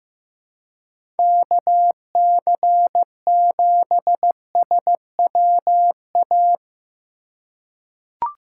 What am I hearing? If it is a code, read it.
Morse code, heard as a keyed tone, about 16 wpm, 700 Hz. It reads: KC7SWA
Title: USA Call Signs - 6 Characters - 2x Character Spacing - 15wpm